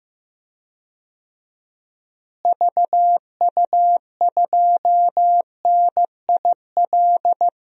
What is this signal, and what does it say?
Morse code, heard as a keyed tone, about 15 wpm, 700 Hz. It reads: VU2NIL
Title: DXCC-100 Call Signs - 6 Characters - Repeated 2x - 15wpm